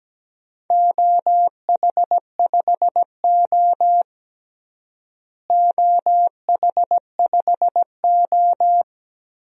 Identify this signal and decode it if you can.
Morse code, heard as a keyed tone, about 17 wpm, 700 Hz. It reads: OH5O OH5O
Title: DXCC-100 Call Signs - 4 Characters - Repeated 2x - 17wpm